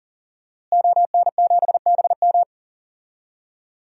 Morse code, heard as a keyed tone, about 40 wpm, 700 Hz. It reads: ON7BM
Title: DXCC-100 Call Signs - 5 Characters - Repeated 3x - 40wpm